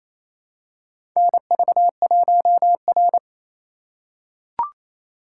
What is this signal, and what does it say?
Morse code, heard as a keyed tone, about 29 wpm, 700 Hz. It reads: NV1R
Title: USA Call Signs - 4 Characters - 4x Character Spacing - 28wpm